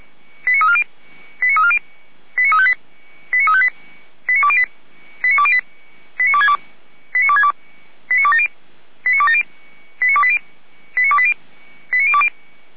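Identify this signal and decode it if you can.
Five-tone selective call (ZVEI), heard as quick runs of five tones, tones of about 70 ms. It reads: 89340 89340 89358 89358 89208 89208 89262 89262 89270 89280 89280 89020
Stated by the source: ZVEI five-tone selective call received on an SDR.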